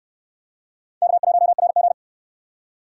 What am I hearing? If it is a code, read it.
Morse code, heard as a keyed tone, about 62 wpm, 700 Hz. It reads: K1RG